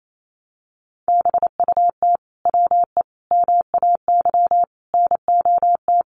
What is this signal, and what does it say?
Morse code, heard as a keyed tone, about 28 wpm, 700 Hz. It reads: BUT WE MAY NOT
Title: Sentences from Top 1500 Words 28wpm